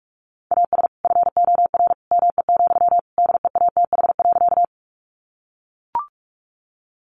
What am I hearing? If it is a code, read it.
Morse code, heard as a keyed tone, about 44 wpm, 700 Hz. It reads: AS FOR ME, DEATH.
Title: Sentences from Top 1100 Words 45wpm